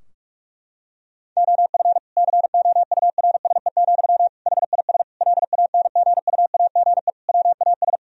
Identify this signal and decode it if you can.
Morse code, heard as a keyed tone, about 46 wpm, 700 Hz. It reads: OF COURSE, HIS LANGUAGE WAS
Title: Sentences from Top 1000 Words - Review Part 3 - 45wpm